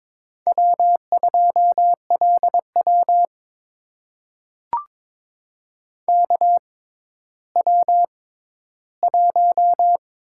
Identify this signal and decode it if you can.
Morse code, heard as a keyed tone, about 22 wpm, spaced as for 12 wpm, 700 Hz. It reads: W2LW KW1
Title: USA Call Signs - 4 Characters - 6x Character Spacing - 22wpm